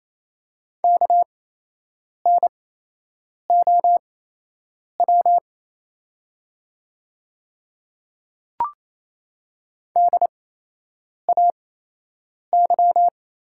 Morse code, heard as a keyed tone, about 28 wpm, spaced as for 8 wpm, 700 Hz. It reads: KNOW DAY
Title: 8x Character Spacing - Top 100 Words - 28wpm